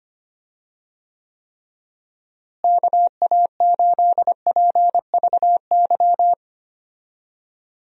Morse code, heard as a keyed tone, about 25 wpm, 700 Hz. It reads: KA8PVY
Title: USA Call Signs - 6 Characters - Repeated 3x - 25wpm